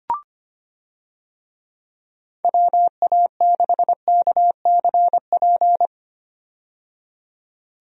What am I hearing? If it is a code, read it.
Morse code, heard as a keyed tone, about 25 wpm, 700 Hz. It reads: WA6KCP
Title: USA Call Signs - 6 Characters - 25wpm